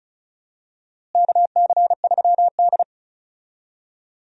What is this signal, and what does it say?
Morse code, heard as a keyed tone, about 36 wpm, 700 Hz. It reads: KC3D
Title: USA Call Signs - 4 Characters - Repeated 3x - 35wpm